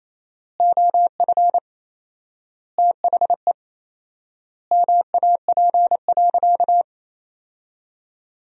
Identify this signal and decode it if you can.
Morse code, heard as a keyed tone, about 28 wpm, 700 Hz. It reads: OF THE MAP.
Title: 4x Word-Spacing - Sentences from Top 400 Words - 28wpm